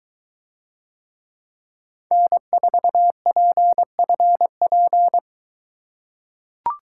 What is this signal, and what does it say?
Morse code, heard as a keyed tone, about 23 wpm, 700 Hz. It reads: N4PFP